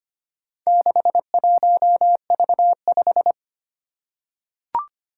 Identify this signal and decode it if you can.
Morse code, heard as a keyed tone, about 25 wpm, 700 Hz. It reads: 61V5
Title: Four Challenging Letter-Number Combinations 25wpm